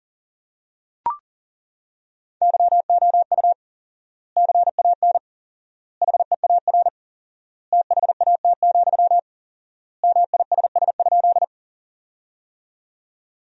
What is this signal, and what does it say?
Morse code, heard as a keyed tone, about 37 wpm, 700 Hz. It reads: YOU CAN HEAR THAT, MISS?
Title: 4x Word-Spacing - Sentences from Top 500 Words - 40wpm